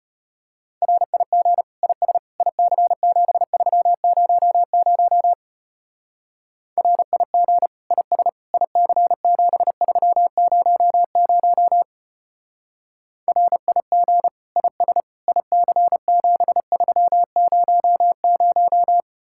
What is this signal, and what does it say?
Morse code, heard as a keyed tone, about 31 wpm, 700 Hz. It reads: RIG IS IC7300 RIG IS IC7300 RIG IS IC7300
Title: Speed-Racing Common QSO Phrases - 25wpm